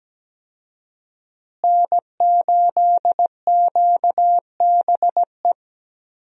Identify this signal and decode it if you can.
Morse code, heard as a keyed tone, about 17 wpm, 700 Hz. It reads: N8QBE
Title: USA Call Signs - 5 Characters - Repeated 2x - 17wpm